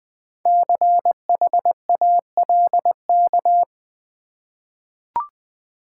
Morse code, heard as a keyed tone, about 20 wpm, 700 Hz. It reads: CHALK
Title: Easy Words with Hints - 20wpm